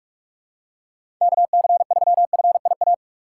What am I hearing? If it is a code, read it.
Morse code, heard as a keyed tone, about 46 wpm, 700 Hz. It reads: KC3FIA
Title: USA Call Signs - 2x Character Spacing - 45wpm